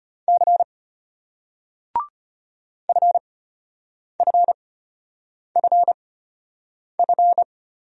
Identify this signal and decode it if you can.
Morse code, heard as a keyed tone, about 31 wpm, spaced as for 8 wpm, 700 Hz. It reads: C FFFF